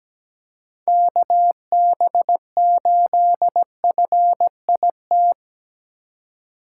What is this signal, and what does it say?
Morse code, heard as a keyed tone, about 17 wpm, 700 Hz. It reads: KB8FIT
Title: USA Call Signs - 6 Characters - 4x Character Spacing - 17wpm